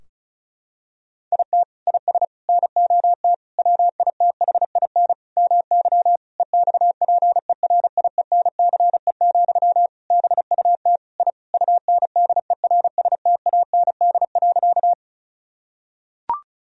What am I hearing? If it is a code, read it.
Morse code, heard as a keyed tone, about 36 wpm, 700 Hz. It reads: IT IS NOT WITHIN MY EXPERIENCE, BUT I UNDERSTAND.
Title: Sentences from Top 1000 Words - Review Part 4 - 35wpm